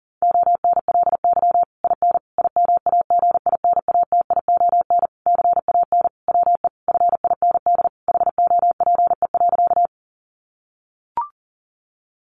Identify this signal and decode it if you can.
Morse code, heard as a keyed tone, about 40 wpm, 700 Hz. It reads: ONLY IN IMAGINATION CAN WE FIND HOPE.
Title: Sentences from Top 1300 Words 40wpm